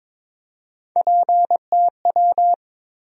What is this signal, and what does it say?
Morse code, heard as a keyed tone, about 22 wpm, 700 Hz. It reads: PTW
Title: USA Call Sign Postfixes - 3 Letters 22wpm